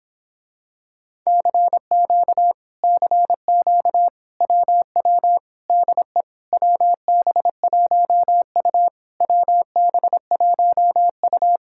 Morse code, heard as a keyed tone, about 26 wpm, 700 Hz. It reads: CQ CQ WW DE WB1U WB1U